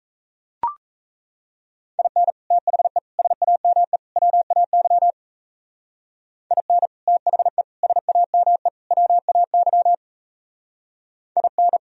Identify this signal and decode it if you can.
Morse code, heard as a keyed tone, about 36 wpm, 700 Hz. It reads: IN THE SAME WAY IN THE SAME WAY IN